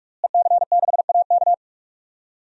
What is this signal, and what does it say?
Morse code, heard as a keyed tone, about 46 wpm, 700 Hz. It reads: EC6AK
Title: DXCC-100 Call Signs - 5 Characters - Repeated 2x - 45wpm